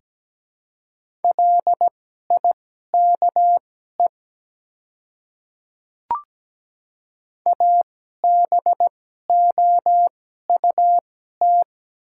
Morse code, heard as a keyed tone, about 17 wpm, spaced as for 12 wpm, 700 Hz. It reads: LIKE ABOUT
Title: 2x Character Spacing - Top 100 Words - 17wpm